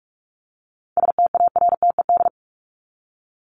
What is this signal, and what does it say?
Morse code, heard as a keyed tone, about 44 wpm, 700 Hz. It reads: STARTED
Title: ED Suffix - Top 100 Words with Base Words 45wpm